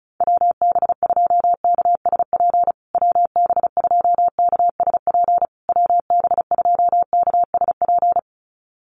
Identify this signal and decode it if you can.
Morse code, heard as a keyed tone, about 34 wpm, 700 Hz. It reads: WB2KSP WB2KSP WB2KSP
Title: USA Call Signs Repeated 3x 35wpm